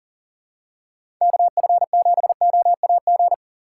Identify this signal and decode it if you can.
Morse code, heard as a keyed tone, about 40 wpm, 700 Hz. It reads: KF7OAG
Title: USA Call Signs - 6 Characters - 4x Character Spacing - 40wpm